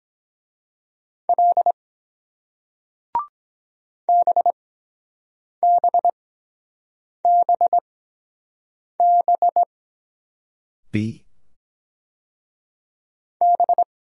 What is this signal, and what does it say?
Morse code, heard as a keyed tone, about 23 wpm, spaced as for 7 wpm, 700 Hz. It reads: L BBBB B